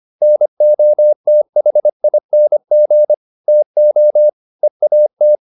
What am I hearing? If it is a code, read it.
Morse code, heard as a keyed tone, about 25 wpm, 600 Hz. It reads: NOTHING TO EAT